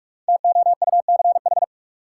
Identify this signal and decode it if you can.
Morse code, heard as a keyed tone, about 45 wpm, 700 Hz. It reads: TOUCH
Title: Sentences from Top 1200 Words - New Word Drill Alt - 45wpm